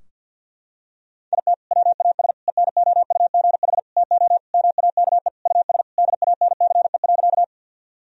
Morse code, heard as a keyed tone, about 46 wpm, 700 Hz. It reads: IT WAS ENOUGH TO MAKE US DANCE.
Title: Sentences from Top 1000 Words - Review Part 4 - 50wpm